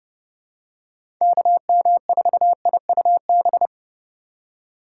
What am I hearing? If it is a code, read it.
Morse code, heard as a keyed tone, about 32 wpm, 700 Hz. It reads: KM4IUB
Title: USA Call Signs - 6 Characters - 2x Character Spacing - 30wpm